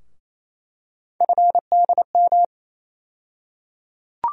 Morse code, heard as a keyed tone, about 28 wpm, 700 Hz. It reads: FDM
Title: USA Call Sign Postfixes - 3 Letters 28wpm